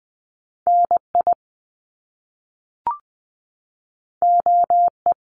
Morse code, heard as a keyed tone, about 19 wpm, 700 Hz. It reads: NI OE
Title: Top Two Letter Combinations in Words 20wpm